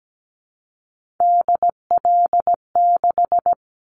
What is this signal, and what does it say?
Morse code, heard as a keyed tone, about 17 wpm, 700 Hz. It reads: DL6
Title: Callsign Prefixes for Top 25 DXCC - 17wpm